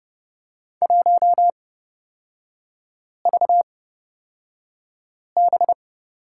Morse code, heard as a keyed tone, about 29 wpm, spaced as for 5 wpm, 700 Hz. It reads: 1VB
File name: Lesson 36e - BT - Three Characters 30wpm